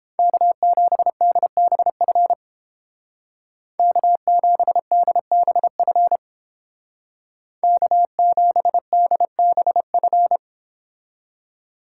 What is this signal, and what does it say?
Morse code, heard as a keyed tone, about 29 wpm, 700 Hz. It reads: K7DBF K7DBF K7DBF